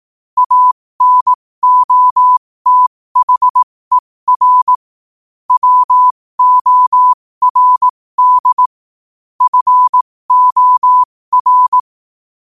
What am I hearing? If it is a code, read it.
Morse code, heard as a keyed tone, about 18 wpm, spaced as for 15 wpm, 1000 Hz. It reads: ANOTHER WORD FOR
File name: qotdfiles / 15wpm